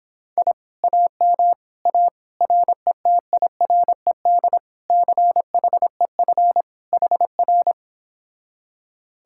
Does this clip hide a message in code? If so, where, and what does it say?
Yes: Morse code from 0.4 to 7.7 s, heard as a keyed tone, about 26 wpm, 700 Hz. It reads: I AM A RETIRED CHEF HR